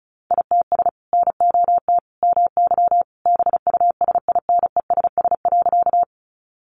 Morse code, heard as a keyed tone, about 34 wpm, 700 Hz. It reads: ITS NOT MY BUSINESS.
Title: Sentences from Top 600 Words 35wpm